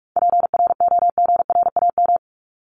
Morse code, heard as a keyed tone, about 44 wpm, 700 Hz. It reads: PROGRAM
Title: ES:S Suffix - Top 100 Words with Base Words 45wpm